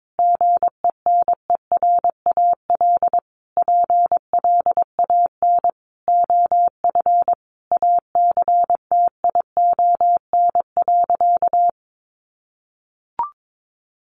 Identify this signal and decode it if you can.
Morse code, heard as a keyed tone, about 22 wpm, 700 Hz. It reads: GENERAL PLAN OF ACTION.
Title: Sentences from Top 700 Words 22wpm